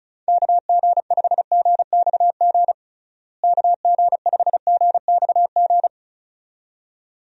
Morse code, heard as a keyed tone, about 36 wpm, 700 Hz. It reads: KG5GXG KG5GXG
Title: USA Call Signs - 6 Characters - Repeated 2x - 35wpm